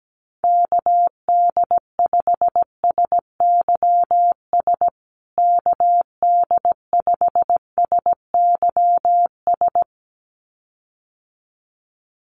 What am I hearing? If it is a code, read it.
Morse code, heard as a keyed tone, about 17 wpm, 700 Hz. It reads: KD5SYS KD5SYS
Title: USA Call Signs Repeated 2x 17wpm